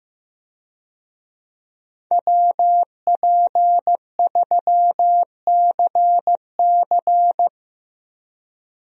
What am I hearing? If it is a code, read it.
Morse code, heard as a keyed tone, about 15 wpm, 700 Hz. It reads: WP3CC
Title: USA Call Signs - 5 Characters - Repeated 2x - 15wpm